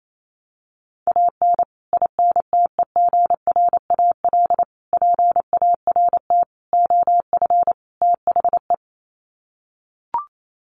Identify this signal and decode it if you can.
Morse code, heard as a keyed tone, about 28 wpm, 700 Hz. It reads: AN INTEGRAL PART OF THE
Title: Sets of 5 Words 28wpm